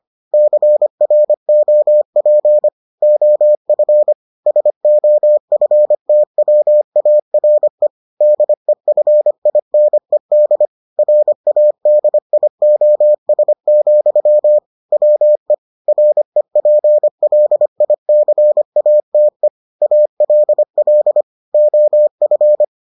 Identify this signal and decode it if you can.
Morse code, heard as a keyed tone, about 25 wpm, 600 Hz. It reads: CROP OF SOFTWARE DEFINED RADIOS, WE REPLICATE ALL OF